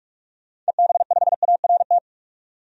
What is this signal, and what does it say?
Morse code, heard as a keyed tone, about 41 wpm, 700 Hz. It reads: EB5ART
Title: DXCC-100 Call Signs - 6 Characters - 45wpm